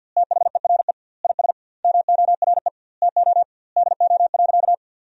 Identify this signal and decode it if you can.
Morse code, heard as a keyed tone, about 50 wpm, 700 Hz. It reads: THERE IS MORE TO DO.
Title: 2x Word-Spacing - Sentences from Top 100 Words - 50wpm